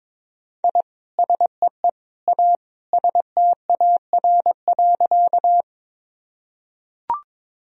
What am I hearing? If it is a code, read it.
Morse code, heard as a keyed tone, about 22 wpm, 700 Hz. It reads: I SEE A STAR.
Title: Sentences from Top 1000 Words - Review Part 2 - 22wpm